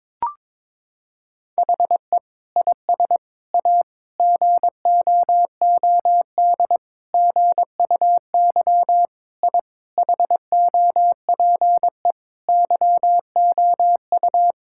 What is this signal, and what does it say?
Morse code, heard as a keyed tone, about 22 wpm, 700 Hz. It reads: HE IS A GOOD GUY I HOPE YOU
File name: practice / 22wpm / Sets of 8 Words - 22wpm